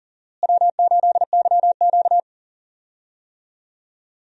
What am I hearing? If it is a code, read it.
Morse code, heard as a keyed tone, about 40 wpm, 700 Hz. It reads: W8YQ
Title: USA Call Signs - 4 Characters - Repeated 3x - 40wpm